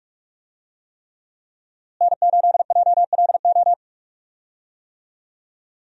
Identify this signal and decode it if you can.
Morse code, heard as a keyed tone, about 44 wpm, 700 Hz. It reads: N8JLO
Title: USA Call Signs - 5 Characters - Repeated 3x - 45wpm